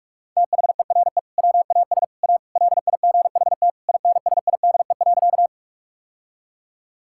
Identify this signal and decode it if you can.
Morse code, heard as a keyed tone, about 48 wpm, 700 Hz. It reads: THERE WAS A LIGHT INSIDE.
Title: Sentences from Top 1000 Words - Review Part 6 - 45wpm